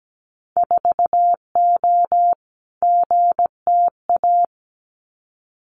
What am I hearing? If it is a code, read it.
Morse code, heard as a keyed tone, about 17 wpm, 700 Hz. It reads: 4O GTA
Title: Winter Field Day Exchanges 17wpm